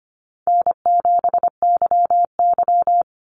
Morse code, heard as a keyed tone, about 25 wpm, 700 Hz. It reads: N7YY
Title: Easy Call Signs - 25wpm